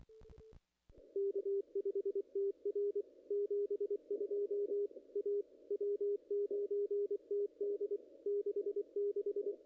Morse code, heard as a keyed tone, about 24 wpm, 410 Hz, with radio noise. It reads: K5TR 72 A W9TD 66
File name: contests / ss-cw-k5tr-op-n5rz -0100